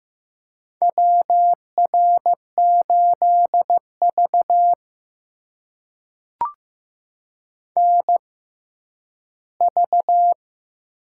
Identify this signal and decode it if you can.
Morse code, heard as a keyed tone, about 15 wpm, spaced as for 10 wpm, 700 Hz. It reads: WR8V NV
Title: USA Call Signs - 4 Characters - 6x Character Spacing - 15wpm